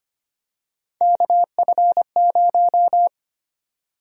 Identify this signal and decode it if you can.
Morse code, heard as a keyed tone, about 25 wpm, 700 Hz. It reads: KF0
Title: USA Call Sign Prefixes - 25wpm